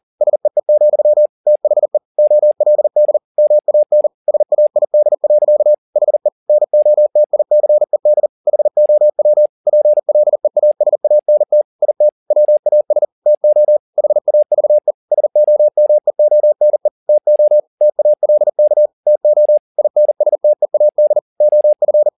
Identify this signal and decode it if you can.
Morse code, heard as a keyed tone, about 40 wpm, 600 Hz. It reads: SEE, THE OLD MAN SAID. HE NOTICED HOW PLEASANT IT WAS TO HAVE SOMEONE TO TALK TO INSTEAD OF
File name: the old man and the sea 2 40wpm